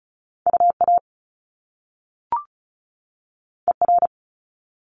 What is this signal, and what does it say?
Morse code, heard as a keyed tone, about 36 wpm, 700 Hz. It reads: UA ER
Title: Top Two Letter Combinations in Words 35wpm